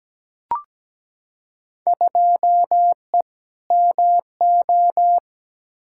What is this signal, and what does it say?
Morse code, heard as a keyed tone, about 17 wpm, 700 Hz. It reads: 2E MO
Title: Field Day Exchanges - 17wpm